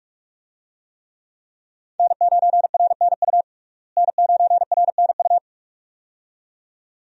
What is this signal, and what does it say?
Morse code, heard as a keyed tone, about 41 wpm, 700 Hz. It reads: N9RNU N9RNU
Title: USA Call Signs - 5 Characters - Repeated 2x - 45wpm